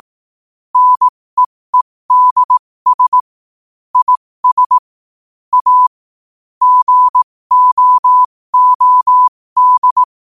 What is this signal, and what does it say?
Morse code, heard as a keyed tone, about 18 wpm, spaced as for 15 wpm, 1000 Hz. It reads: NEEDS IS A GOOD
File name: qotdfiles / 15wpm